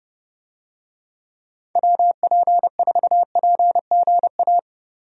Morse code, heard as a keyed tone, about 30 wpm, 700 Hz. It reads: WP4PGA
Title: USA Call Signs - 6 Characters - 30wpm